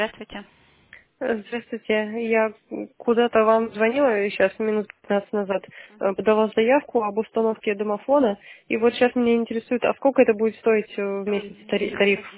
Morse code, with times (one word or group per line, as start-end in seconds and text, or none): none